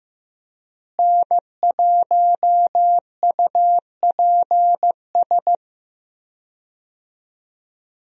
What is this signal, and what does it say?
Morse code, heard as a keyed tone, about 15 wpm, 700 Hz. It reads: N1UPS
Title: USA Call Signs - 5 Characters - Repeated 3x - 15wpm